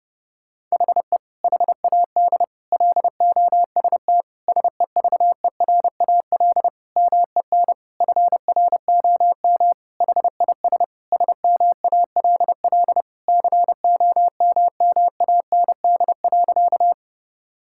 Morse code, heard as a keyed tone, about 30 wpm, 700 Hz. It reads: HE HAD LOST SEVERAL MEN FROM HIS SMALL COMMAND.
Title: Sentences from Top 1600 Words 30wpm